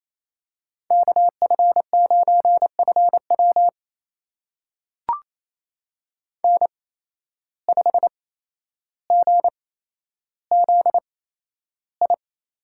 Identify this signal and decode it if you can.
Morse code, heard as a keyed tone, about 28 wpm, spaced as for 12 wpm, 700 Hz. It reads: KF9FW N5GZI
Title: USA Call Signs - 5 Characters - 8x Character Spacing - 28wpm